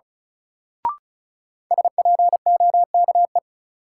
Morse code, heard as a keyed tone, about 34 wpm, 700 Hz. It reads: SPOKE
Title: Sentences from Top 1200 Words - New Word Drill - 35wpm